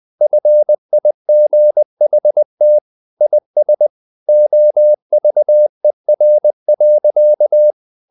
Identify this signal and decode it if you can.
Morse code, heard as a keyed tone, about 20 wpm, 600 Hz. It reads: FIGHT IS OVER.
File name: the old man and the sea part 2 20wpm